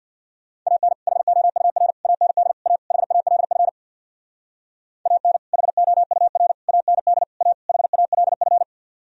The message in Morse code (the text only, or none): AN HOUR AND A HALF AN HOUR AND A HALF